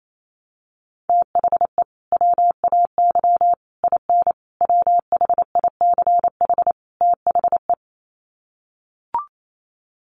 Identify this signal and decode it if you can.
Morse code, heard as a keyed tone, about 28 wpm, 700 Hz. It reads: THE WAY IN WHICH THE
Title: Sets of 5 Words Encore2 28wpm